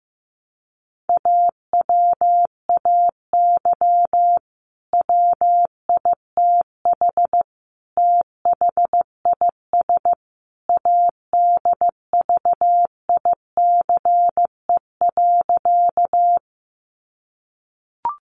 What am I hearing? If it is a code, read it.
Morse code, heard as a keyed tone, about 15 wpm, 700 Hz. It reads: AWAY WITH THIS ADVICE.
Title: Sentences from Top 1100 Words 15wpm